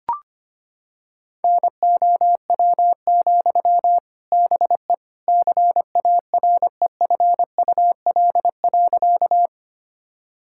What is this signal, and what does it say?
Morse code, heard as a keyed tone, about 25 wpm, 700 Hz. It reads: NOW, BE CAREFUL.
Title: Sentences from Top 1600 Words 25wpm